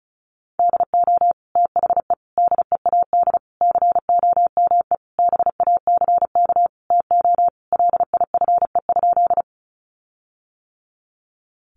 Morse code, heard as a keyed tone, about 34 wpm, 700 Hz. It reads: DO THE DEAD COME BACK TO LIFE?
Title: Sentences from Top 1500 Words 35wpm